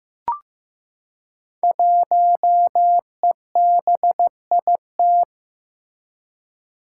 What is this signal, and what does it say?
Morse code, heard as a keyed tone, about 15 wpm, 700 Hz. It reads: 1EBIT
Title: Five Challenging Letter-Number Combinations 15wpm